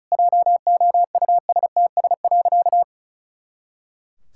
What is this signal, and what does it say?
Morse code, heard as a keyed tone, about 34 wpm, 700 Hz. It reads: JOUSTS.
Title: Pangram Challenge v2 - 35wpm